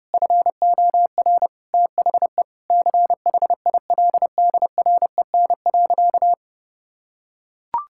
Morse code, heard as a keyed tone, about 30 wpm, 700 Hz. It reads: FOR THE CHILDREN.
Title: Sentences from Top 1000 Words - Review Part 6 - 30wpm